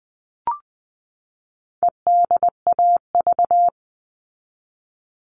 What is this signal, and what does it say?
Morse code, heard as a keyed tone, about 20 wpm, 700 Hz. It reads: EDAV
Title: All Cut Numbers - 4-Digit 20wpm